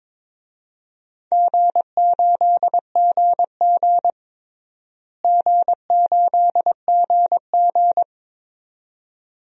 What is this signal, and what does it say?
Morse code, heard as a keyed tone, about 22 wpm, 700 Hz. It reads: G8GG G8GG
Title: DXCC-100 Call Signs - 4 Characters - Repeated 2x - 22wpm